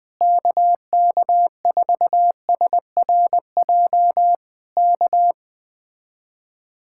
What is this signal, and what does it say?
Morse code, heard as a keyed tone, about 20 wpm, 700 Hz. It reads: KK4SRJ K